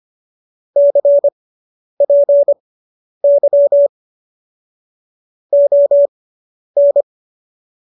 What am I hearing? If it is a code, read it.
Morse code, heard as a keyed tone, about 25 wpm, spaced as for 10 wpm, 570 Hz. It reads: CPY ON